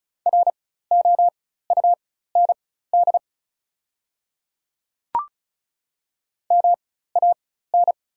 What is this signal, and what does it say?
Morse code, heard as a keyed tone, about 36 wpm, spaced as for 16 wpm, 700 Hz. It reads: ROUND MAN
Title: 4x Character Spacing - Top 500 Words - 35wpm